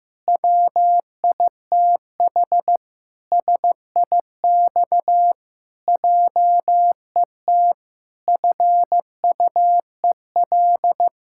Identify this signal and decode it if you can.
Morse code, heard as a keyed tone, about 15 wpm, 700 Hz. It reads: WITH SIX JET FUEL
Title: Pangram Challenge v2 - 15wpm